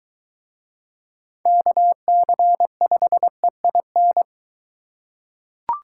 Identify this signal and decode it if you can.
Morse code, heard as a keyed tone, about 22 wpm, 700 Hz. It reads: KC5EIN